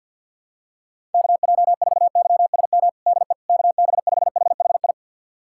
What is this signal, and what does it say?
Morse code, heard as a keyed tone, about 49 wpm, 700 Hz. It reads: KJ4YSM DE KB5HHI
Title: USA Callsign DE Callsign - 6 Characters - 50wpm